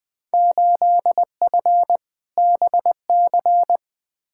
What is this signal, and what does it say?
Morse code, heard as a keyed tone, about 20 wpm, 700 Hz. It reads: 8F BC
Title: Field Day Exchanges Encore - 20wpm